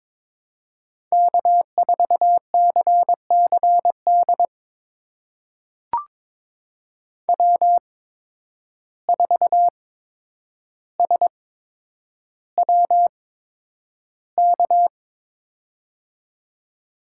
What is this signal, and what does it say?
Morse code, heard as a keyed tone, about 22 wpm, spaced as for 9 wpm, 700 Hz. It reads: K4CCD W4SWK
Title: USA Call Signs - 5 Characters - 8x Character Spacing - 22wpm